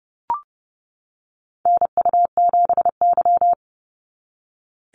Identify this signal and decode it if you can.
Morse code, heard as a keyed tone, about 30 wpm, 700 Hz. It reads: NU7Y
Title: NA SOTA Activator Callsigns 30wpm